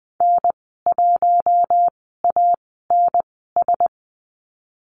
Morse code, heard as a keyed tone, about 20 wpm, spaced as for 15 wpm, 700 Hz. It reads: N1ANS
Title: 048 Lesson 9g - Number 4 - Call Signs 20wpm